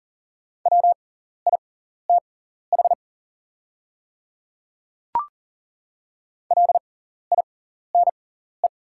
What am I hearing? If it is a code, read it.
Morse code, heard as a keyed tone, about 40 wpm, spaced as for 14 wpm, 700 Hz. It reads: WITH LINE
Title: 6x Character Spacing - Top 500 Words - 40wpm